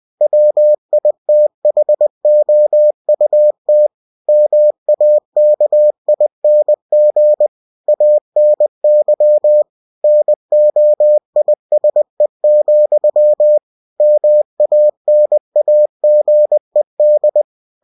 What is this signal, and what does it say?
Morse code, heard as a keyed tone, about 20 wpm, 600 Hz. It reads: WITHOUT MAKING ANY NOISE, MANAGED
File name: cw-books / winnie-the-pooh / winnie-20wpm-page-0064